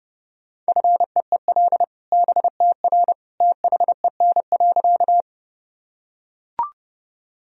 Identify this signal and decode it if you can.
Morse code, heard as a keyed tone, about 30 wpm, 700 Hz. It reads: FEEL BTR THEN.
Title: Sentences from Top 400 Words and Abbreviations 30wpm